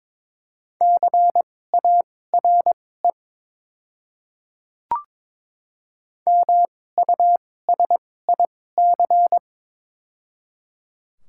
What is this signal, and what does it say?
Morse code, heard as a keyed tone, about 22 wpm, spaced as for 16 wpm, 700 Hz. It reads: CARE MUSIC
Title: 2x Character Spacing - Top 300 Words - 22wpm